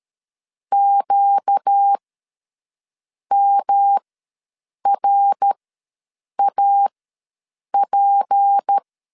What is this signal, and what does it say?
Morse code, heard as a keyed tone, about 13 wpm, spaced as for 6 wpm, 800 Hz. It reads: QMRAP
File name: LEZIONI / 6 PKX0 / 6 P ws 12wpm 3117